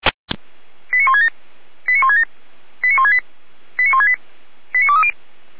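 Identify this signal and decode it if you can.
Five-tone selective call (ZVEI), heard as quick runs of five tones, tones of about 80 ms. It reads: 89167 89167 89168 89168 89230
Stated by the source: ZVEI five-tone selective call received on SDR.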